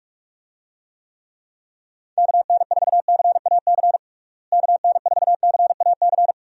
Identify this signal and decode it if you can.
Morse code, heard as a keyed tone, about 46 wpm, 700 Hz. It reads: KN4CAC KN4CAC
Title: USA Call Signs - 6 Characters - Repeated 2x - 45wpm